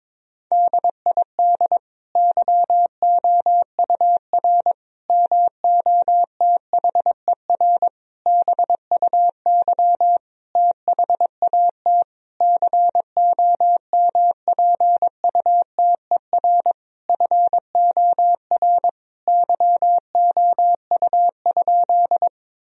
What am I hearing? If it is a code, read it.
Morse code, heard as a keyed tone, about 22 wpm, 700 Hz. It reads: DID YOUR MOTHER BUY THAT COMPUTER FOR YOU?
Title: Sentences from Top 1000 Words - Review Part 3 - 22wpm